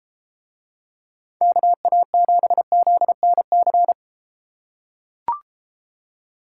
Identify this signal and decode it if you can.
Morse code, heard as a keyed tone, about 30 wpm, 700 Hz. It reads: KA7ZNC